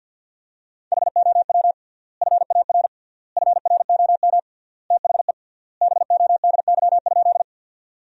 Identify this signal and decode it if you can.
Morse code, heard as a keyed tone, about 47 wpm, 700 Hz. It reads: HOW FAR FROM THE BODY?
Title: 3x Word-Spacing - Sentences from Top 400 Words - 50wpm